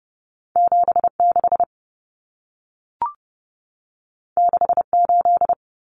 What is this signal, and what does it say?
Morse code, heard as a keyed tone, about 30 wpm, 700 Hz. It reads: 76 68
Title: Numbers - 2-Digit 30wpm